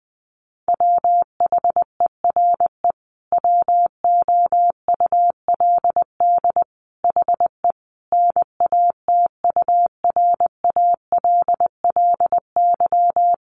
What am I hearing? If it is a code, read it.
Morse code, heard as a keyed tone, about 20 wpm, 700 Hz. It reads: WHERE WOULD HE NATURALLY
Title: Sentences from Top 1300 Words 20wpm